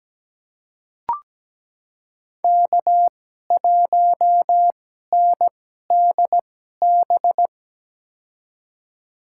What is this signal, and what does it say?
Morse code, heard as a keyed tone, about 17 wpm, spaced as for 12 wpm, 700 Hz. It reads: K1NDB
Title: USA Call Signs - 5 Characters - 2x Character Spacing - 17wpm